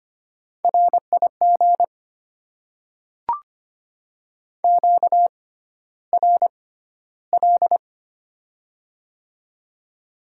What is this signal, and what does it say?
Morse code, heard as a keyed tone, about 25 wpm, spaced as for 13 wpm, 700 Hz. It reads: RIG QRL
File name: Top 20 QSO Elements - 6x Character Spacing - 25wpm